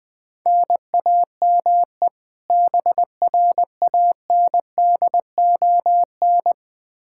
Morse code, heard as a keyed tone, about 20 wpm, 700 Hz. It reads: NAME BRANDON
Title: K1USN SST QSO Practice 20wpm